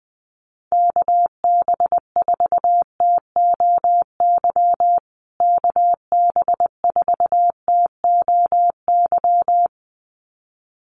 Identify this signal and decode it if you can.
Morse code, heard as a keyed tone, about 20 wpm, 700 Hz. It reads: KB4TOY KB4TOY
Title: USA Call Signs Repeated 2x 20wpm